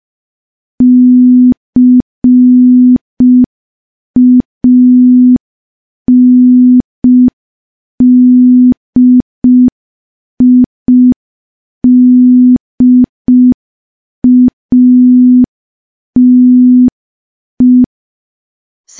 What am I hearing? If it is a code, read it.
Morse code, heard as a keyed tone, about 5 wpm, 250 Hz. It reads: CANDIDATE
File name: morse / morse-training-05WPM